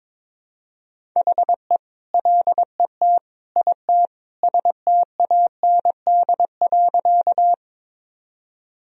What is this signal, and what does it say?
Morse code, heard as a keyed tone, about 22 wpm, 700 Hz. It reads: HE LET IT STAND.
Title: Sentences from Top 1000 Words - Review Part 1 - 22wpm